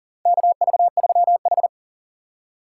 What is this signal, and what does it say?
Morse code, heard as a keyed tone, about 37 wpm, 700 Hz. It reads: KV3H
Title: USA Call Signs - 4x Character Spacing - 40wpm